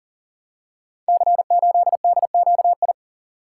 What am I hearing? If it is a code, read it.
Morse code, heard as a keyed tone, about 40 wpm, 700 Hz. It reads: C8DQI
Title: Code Groups - 5 Letter:Numbers - 40wpm